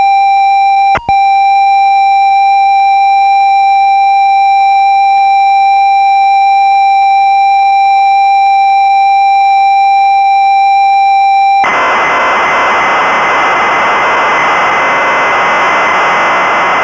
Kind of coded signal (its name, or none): computer data on cassette